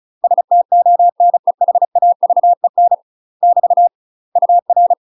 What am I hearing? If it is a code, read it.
Morse code, heard as a keyed tone, about 36 wpm, 700 Hz. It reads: STONEHAVEN <BT> UR
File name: UK MASTER QSOs TRAINING / 35WPM / UK MASTER QSOs 0009